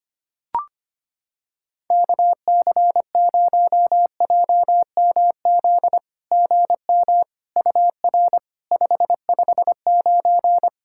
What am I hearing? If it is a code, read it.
Morse code, heard as a keyed tone, about 25 wpm, 700 Hz. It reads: KC0JMZ GM UR 559
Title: SOTA QSO Practice 25wpm